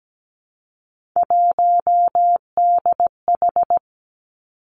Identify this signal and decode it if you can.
Morse code, heard as a keyed tone, about 17 wpm, 700 Hz. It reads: 1DH
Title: Three Challenging Letter-Number Combinations 17wpm